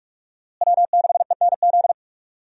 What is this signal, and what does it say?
Morse code, heard as a keyed tone, about 44 wpm, 700 Hz. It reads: W6ENZ
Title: USA Call Signs - 5 Characters - Repeated 3x - 45wpm